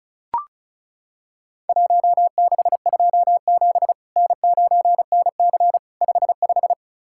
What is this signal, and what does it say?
Morse code, heard as a keyed tone, about 36 wpm, 700 Hz. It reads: 1627 N9NC 55
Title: Worked All Europe DX Contest - QTC Exchange - 35wpm